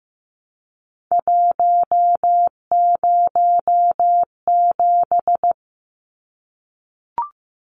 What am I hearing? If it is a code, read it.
Morse code, heard as a keyed tone, about 15 wpm, 700 Hz. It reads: 107
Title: Numbers - 3-Digit 15wpm